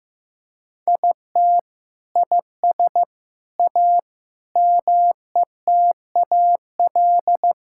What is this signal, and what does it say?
Morse code, heard as a keyed tone, about 16 wpm, 700 Hz. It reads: IT IS A METAL
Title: Sentences from Top 1000 Words - Review Part 6 - 15wpm